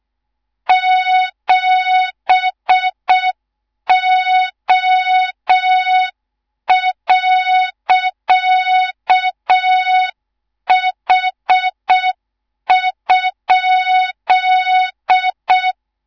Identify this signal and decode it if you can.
Morse code, heard as a keyed tone, about 6 wpm, 3700 Hz, with noise. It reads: 7O.H?
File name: Morze-HA2MN / hang / 18 IRASJELEK2